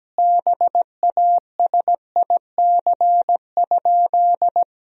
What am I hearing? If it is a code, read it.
Morse code, heard as a keyed tone, about 17 wpm, 700 Hz. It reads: BASIC?
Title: Sentences from Top 1000 Words - Review Part 3 - 17wpm